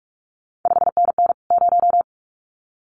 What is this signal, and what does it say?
Morse code, heard as a keyed tone, about 46 wpm, 700 Hz. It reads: 5NN 0
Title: CQ WW DX Contest Exchange Encore - 45wpm